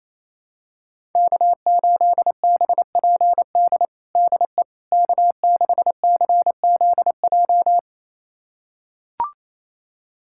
Morse code, heard as a keyed tone, about 28 wpm, 700 Hz. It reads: K8BPD DE K6CZJ
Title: USA Callsign DE Callsign - 5 Characters - 28wpm